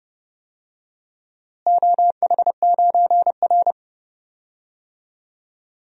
Morse code, heard as a keyed tone, about 29 wpm, 700 Hz. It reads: OH9R
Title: DXCC-100 Call Signs - 4 Characters - Repeated 3x - 30wpm